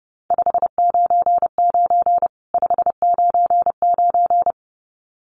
Morse code, heard as a keyed tone, about 30 wpm, 700 Hz. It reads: 599 599
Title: Common QSO Phrases 30wpm